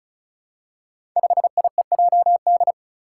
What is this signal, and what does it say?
Morse code, heard as a keyed tone, about 34 wpm, 700 Hz. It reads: 5IEJD
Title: Five Challenging Letter-Number Combinations 35wpm